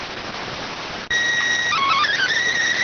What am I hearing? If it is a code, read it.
SSTV (slow-scan television), PD 290.